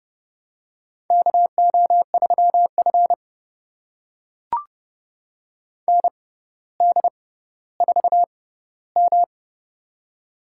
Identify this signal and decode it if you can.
Morse code, heard as a keyed tone, about 30 wpm, spaced as for 15 wpm, 700 Hz. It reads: KO3F ND4M
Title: USA Call Signs - 4 Characters - 6x Character Spacing - 30wpm